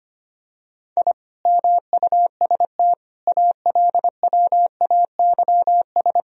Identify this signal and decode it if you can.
Morse code, heard as a keyed tone, about 25 wpm, 700 Hz. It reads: I MUST ALWAYS
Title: Sentences from Top 1000 Words - Review Part 3 - 25wpm